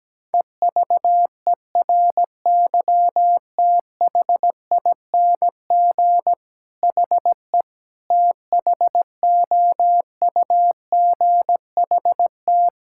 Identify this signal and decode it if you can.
Morse code, heard as a keyed tone, about 17 wpm, 700 Hz. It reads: EVERYTHING HE THOUGHT
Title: Sentences from Top 1000 Words - Review Part 6 - 17wpm